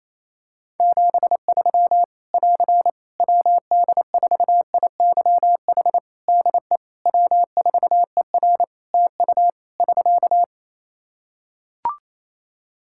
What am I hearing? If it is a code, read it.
Morse code, heard as a keyed tone, about 28 wpm, 700 Hz. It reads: 73 <AR> WD4IYH DE W4ER TU <SK>
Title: Practice QSO - Exchange 4 - Ending - 28wpm